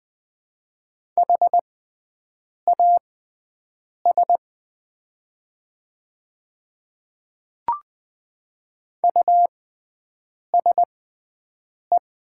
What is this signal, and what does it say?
Morse code, heard as a keyed tone, about 20 wpm, spaced as for 7 wpm, 700 Hz. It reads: HAS USE
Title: 6x Character Spacing - Top 100 Words - 20wpm